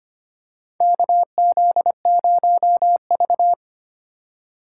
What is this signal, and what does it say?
Morse code, heard as a keyed tone, about 25 wpm, 700 Hz. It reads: KZ0V
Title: USA Call Signs - 4 Characters - 4x Character Spacing - 25wpm